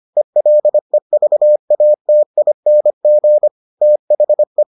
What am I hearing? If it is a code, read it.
Morse code, heard as a keyed tone, about 25 wpm, 600 Hz. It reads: ELEVATING THE